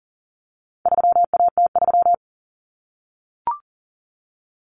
Morse code, heard as a keyed tone, about 40 wpm, 700 Hz. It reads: 3AT3
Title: Common Cut Numbers - 4-Digit 40wpm